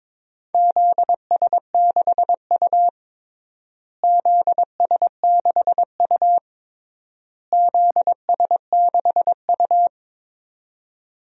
Morse code, heard as a keyed tone, about 22 wpm, 700 Hz. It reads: ZS6U ZS6U ZS6U
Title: DXCC-100 Call Signs - 4 Characters - Repeated 3x - 22wpm